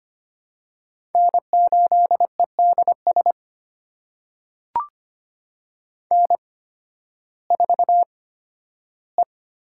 Morse code, heard as a keyed tone, about 25 wpm, spaced as for 13 wpm, 700 Hz. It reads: N8EDS N4E
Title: USA Call Signs - 5 Characters - 8x Character Spacing - 25wpm